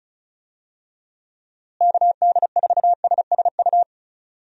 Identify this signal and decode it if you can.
Morse code, heard as a keyed tone, about 34 wpm, 700 Hz. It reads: KD4SSU
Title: USA Call Signs - 6 Characters - Repeated 3x - 35wpm